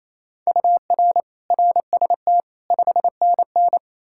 Morse code, heard as a keyed tone, about 28 wpm, 700 Hz. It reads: UR RST 5NN
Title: K1USN SST QSO Practice 28wpm